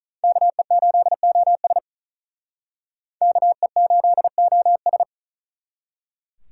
Morse code, heard as a keyed tone, about 43 wpm, 700 Hz. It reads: KE8OS KE8OS
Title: Speed Racing - USA 2x2 Call Signs 35wpm